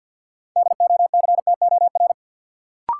Morse code, heard as a keyed tone, about 46 wpm, 700 Hz. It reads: DOCTOR
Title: Sentences from Top 1500 Words - New Word Drill - 50wpm